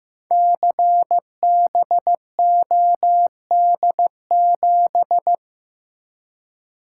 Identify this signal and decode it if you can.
Morse code, heard as a keyed tone, about 15 wpm, 700 Hz. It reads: CBOD7
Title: Code Groups - 5 Letter:Numbers - 15wpm